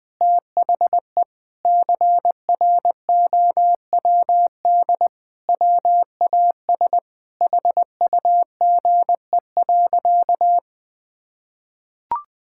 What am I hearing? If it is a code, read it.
Morse code, heard as a keyed tone, about 20 wpm, 700 Hz. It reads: THE CROWD WAS HUGE.
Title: Sentences with 6-Word Phrases - Spoken Beginning - 20wpm